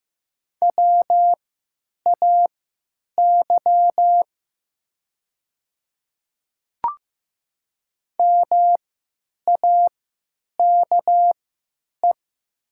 Morse code, heard as a keyed tone, about 15 wpm, spaced as for 9 wpm, 700 Hz. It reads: WAY MAKE
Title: 3x Character Spacing - Top 200 Words - 15wpm